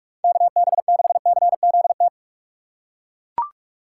Morse code, heard as a keyed tone, about 41 wpm, 700 Hz. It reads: KB6CZT
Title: USA Call Signs - 6 Characters - 2x Character Spacing - 45wpm